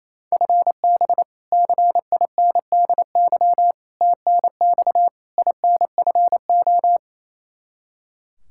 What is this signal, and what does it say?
Morse code, heard as a keyed tone, about 28 wpm, 700 Hz. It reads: FB CINDY TNX INFO
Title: Practice QSO - Exchange 3 - More Info - 28wpm